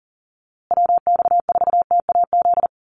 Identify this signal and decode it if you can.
Morse code, heard as a keyed tone, about 40 wpm, 700 Hz. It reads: WX4TAZ
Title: USA Call Signs Repeated 2x 40wpm